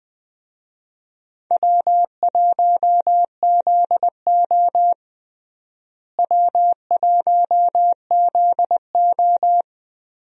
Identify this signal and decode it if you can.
Morse code, heard as a keyed tone, about 20 wpm, 700 Hz. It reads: W1ZO W1ZO
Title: USA Call Signs - 4 Characters - Repeated 2x - 20wpm